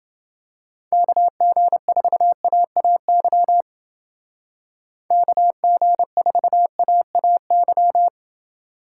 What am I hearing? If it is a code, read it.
Morse code, heard as a keyed tone, about 28 wpm, 700 Hz. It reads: KG4AAY KG4AAY